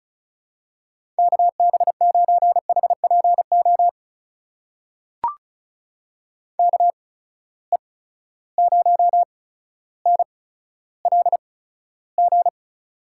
Morse code, heard as a keyed tone, about 36 wpm, spaced as for 15 wpm, 700 Hz. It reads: KB9HPO KE0NLG
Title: USA Call Signs - 6 Characters - 8x Character Spacing - 35wpm